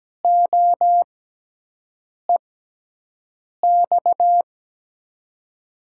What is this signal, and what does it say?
Morse code, heard as a keyed tone, about 17 wpm, spaced as for 6 wpm, 700 Hz. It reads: OEX
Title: USA Call Signs - 6 Characters - 6x Character Spacing - 17wpm